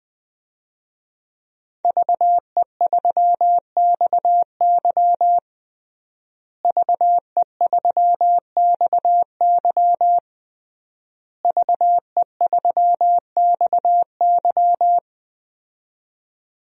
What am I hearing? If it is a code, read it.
Morse code, heard as a keyed tone, about 20 wpm, 700 Hz. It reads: VE3XY VE3XY VE3XY
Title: DXCC-100 Call Signs - 5 Characters - Repeated 3x - 20wpm